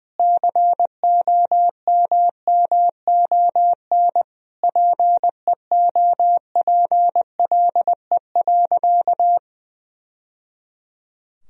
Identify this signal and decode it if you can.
Morse code, heard as a keyed tone, about 20 wpm, 700 Hz. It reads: COMMON PEOPLE.
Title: Sentences from Top 1000 Words - Review Part 3 - 20wpm